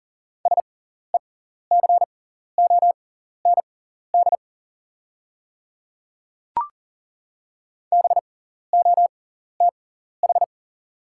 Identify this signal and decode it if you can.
Morse code, heard as a keyed tone, about 39 wpm, spaced as for 14 wpm, 690 Hz. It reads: SECOND BOTH
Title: 6x Character Spacing - Top 300 Words - 40wpm